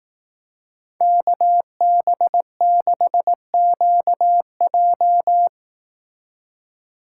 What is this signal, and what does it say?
Morse code, heard as a keyed tone, about 18 wpm, 700 Hz. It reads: KB6QJ
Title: Speed Racing - USA 2x2 Call Signs 15wpm